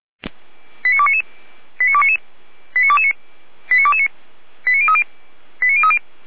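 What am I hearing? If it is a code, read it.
Five-tone selective call (ZVEI), heard as quick runs of five tones, tones of about 70 ms. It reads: 89299 89299 89229 89229 89030 89030
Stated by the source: ZVEI five-tone selective call received via SDR.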